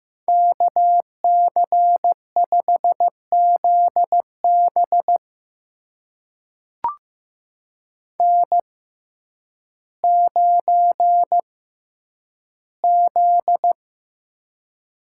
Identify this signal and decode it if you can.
Morse code, heard as a keyed tone, about 15 wpm, spaced as for 9 wpm, 700 Hz. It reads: KC5ZB N9Z
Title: USA Call Signs - 5 Characters - 6x Character Spacing - 15wpm